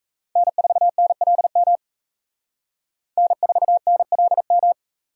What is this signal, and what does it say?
Morse code, heard as a keyed tone, about 35 wpm, 700 Hz. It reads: N4NLM N4NLM